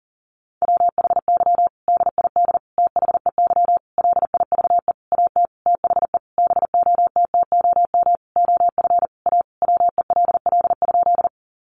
Morse code, heard as a keyed tone, about 40 wpm, 700 Hz. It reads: WHY DID THEY LIVE AT THE BOTTOM OF A WELL?
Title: Sentences from Top 900 Words 40wpm